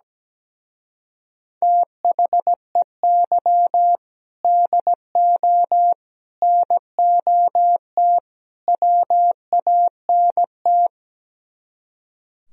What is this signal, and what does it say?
Morse code, heard as a keyed tone, about 17 wpm, 700 Hz. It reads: THEY DO NOT WANT
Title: Speed-Racing - Sets of 4 Words 17wpm